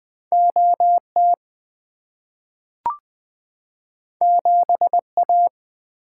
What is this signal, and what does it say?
Morse code, heard as a keyed tone, about 20 wpm, 700 Hz. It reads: OT 7A
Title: Two Letter-Number Combinations - 20wpm